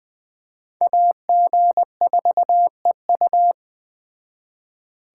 Morse code, heard as a keyed tone, about 20 wpm, 700 Hz. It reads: AG4EU
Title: Speed Racing - USA 2x2 Call Signs 17wpm